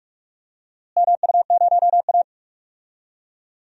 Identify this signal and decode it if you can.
Morse code, heard as a keyed tone, about 46 wpm, 700 Hz. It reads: MU0A
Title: DXCC-100 Call Signs - 4 Characters - Repeated 2x - 45wpm